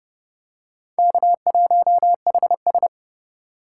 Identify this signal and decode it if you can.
Morse code, heard as a keyed tone, about 30 wpm, 700 Hz. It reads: K1HS